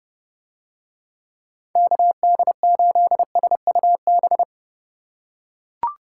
Morse code, heard as a keyed tone, about 30 wpm, 700 Hz. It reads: KD8SUB